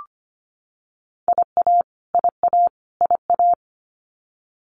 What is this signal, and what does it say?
Morse code, heard as a keyed tone, about 25 wpm, 700 Hz. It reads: IA IA IA
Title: US States Repeated 3x 25wpm